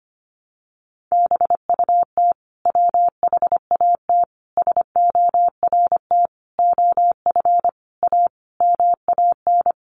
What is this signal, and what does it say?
Morse code, heard as a keyed tone, about 25 wpm, 700 Hz. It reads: BUT WHAT SORT OF A MAN
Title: Sentences from Top 1300 Words 25wpm